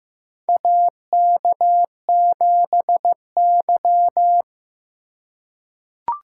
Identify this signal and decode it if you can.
Morse code, heard as a keyed tone, about 15 wpm, 700 Hz. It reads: AK7Y
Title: USA Call Signs - 4 Characters - 3x Character Spacing - 15wpm